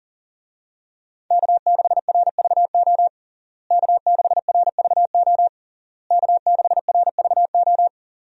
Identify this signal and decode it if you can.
Morse code, heard as a keyed tone, about 40 wpm, 700 Hz. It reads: K6RVO K6RVO K6RVO
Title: USA Call Signs - 5 Characters - Repeated 3x - 40wpm